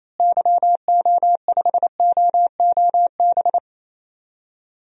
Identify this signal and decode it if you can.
Morse code, heard as a keyed tone, about 27 wpm, 700 Hz. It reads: YO5OOB
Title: DXCC-100 Call Signs - 6 Characters - Repeated 3x - 28wpm